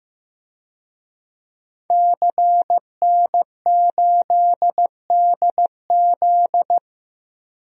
Morse code, heard as a keyed tone, about 15 wpm, 700 Hz. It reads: CN8DZ
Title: DXCC-100 Call Signs - 5 Characters - 15wpm